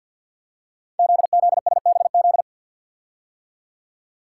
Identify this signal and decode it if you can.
Morse code, heard as a keyed tone, about 48 wpm, 700 Hz. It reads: ZZSBZ
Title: Code Groups - 5 Letters - 50wpm